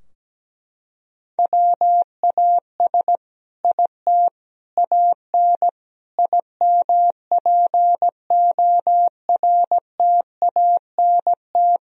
Morse code, heard as a keyed tone, about 17 wpm, 700 Hz. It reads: WAS IT AN IMPORTANT
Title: Sentences from Top 1000 Words - Review Part 3 - 17wpm